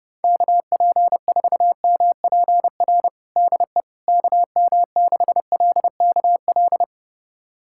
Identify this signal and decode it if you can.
Morse code, heard as a keyed tone, about 31 wpm, 700 Hz. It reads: KP4MPR DE KM6LKL
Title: USA Callsign DE Callsign - 6 Characters - 30wpm